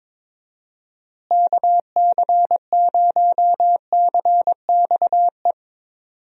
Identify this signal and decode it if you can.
Morse code, heard as a keyed tone, about 22 wpm, 700 Hz. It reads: KC0CXE
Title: USA Call Signs - 6 Characters - Repeated 2x - 22wpm